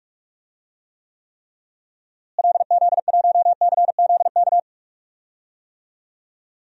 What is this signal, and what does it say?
Morse code, heard as a keyed tone, about 46 wpm, 700 Hz. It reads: LZ1CZK
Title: DXCC-100 Call Signs - 6 Characters - Repeated 3x - 45wpm